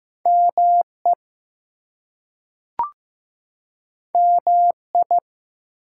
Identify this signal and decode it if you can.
Morse code, heard as a keyed tone, about 15 wpm, 700 Hz. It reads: ME MI
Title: US States and Canadian Provinces 15wpm